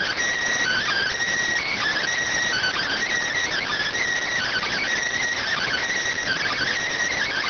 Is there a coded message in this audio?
Yes: Emergency Alert System (SAME) tones (header bursts) from 0.6 to 6.7 s, heard as harsh buzzing bursts.